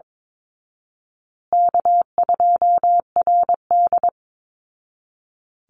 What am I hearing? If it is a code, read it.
Morse code, heard as a keyed tone, about 22 wpm, 700 Hz. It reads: K2RD
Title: Easy Call Signs Repeated 3x - 22wpm